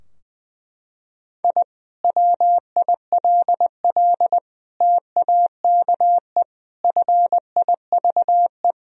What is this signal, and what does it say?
Morse code, heard as a keyed tone, about 20 wpm, 700 Hz. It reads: I WILL TAKE FIVE
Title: Sentences from Top 1000 Words - Review Part 2 - 20wpm